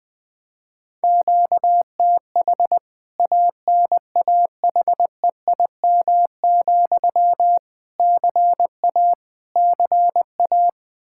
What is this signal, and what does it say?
Morse code, heard as a keyed tone, about 21 wpm, 700 Hz. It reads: QTH ANAHEIM, CA CA
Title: K1USN SST QSO Practice 20wpm